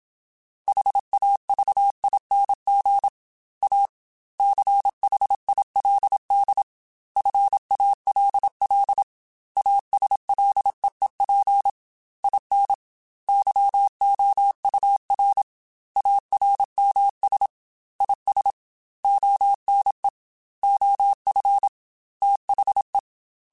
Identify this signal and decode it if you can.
Morse code, heard as a keyed tone, about 27 wpm, 800 Hz. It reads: HAVING A CHILD FALL ASLEEP IN YOUR ARMS IS ONE OF THE
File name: Quotes / Quotes-mp3s / ChildAsleep @25